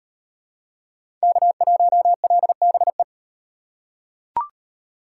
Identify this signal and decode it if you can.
Morse code, heard as a keyed tone, about 34 wpm, 700 Hz. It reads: K1LBE